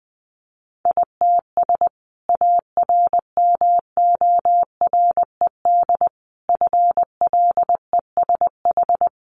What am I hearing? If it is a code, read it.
Morse code, heard as a keyed tone, about 19 wpm, 700 Hz. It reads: ITS ARMORED FLESH
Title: A Sound Of Thunder 20wpm